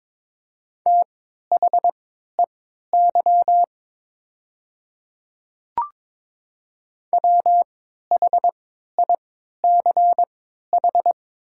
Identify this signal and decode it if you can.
Morse code, heard as a keyed tone, about 21 wpm, spaced as for 12 wpm, 700 Hz. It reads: THEY WHICH
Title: 3x Character Spacing - Top 100 Words - 22wpm